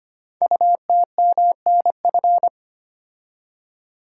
Morse code, heard as a keyed tone, about 25 wpm, 700 Hz. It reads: UTMNF
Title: Code Groups - 5 Letters - 25wpm